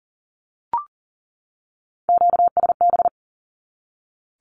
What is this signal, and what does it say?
Morse code, heard as a keyed tone, about 40 wpm, 700 Hz. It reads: QSB
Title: Q-Codes 40wpm